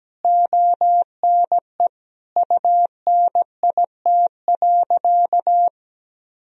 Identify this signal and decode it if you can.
Morse code, heard as a keyed tone, about 17 wpm, 700 Hz. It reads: ONE UNIT.
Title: Sentences from Top 1000 Words - Review Part 2 - 17wpm